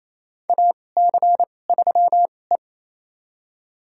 Morse code, heard as a keyed tone, about 28 wpm, spaced as for 20 wpm, 700 Hz. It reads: AC3E
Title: USA Call Signs - 4 Characters - 2x Character Spacing - 28wpm